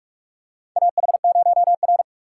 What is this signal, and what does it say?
Morse code, heard as a keyed tone, about 46 wpm, 700 Hz. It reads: AH0R
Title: USA Call Signs - 4 Characters - 3x Character Spacing - 45wpm